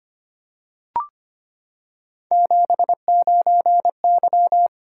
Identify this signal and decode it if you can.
Morse code, heard as a keyed tone, about 25 wpm, 700 Hz. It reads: 79Y
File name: Three Letter-Number Combinations - 25wpm